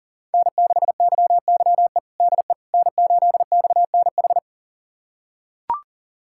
Morse code, heard as a keyed tone, about 40 wpm, 700 Hz. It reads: N6YYE DE N8XNH
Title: USA Callsign DE Callsign - 5 Characters - 40wpm